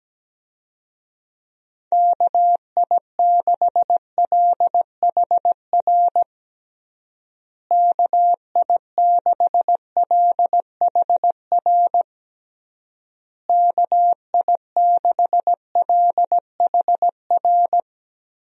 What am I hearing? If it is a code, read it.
Morse code, heard as a keyed tone, about 17 wpm, 700 Hz. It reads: KI6LHR KI6LHR KI6LHR
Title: USA Call Signs - 6 Characters - Repeated 3x - 17wpm